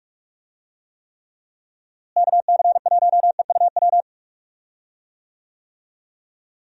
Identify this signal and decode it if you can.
Morse code, heard as a keyed tone, about 46 wpm, 700 Hz. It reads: KC1EUW
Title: USA Call Signs - 6 Characters - Repeated 3x - 45wpm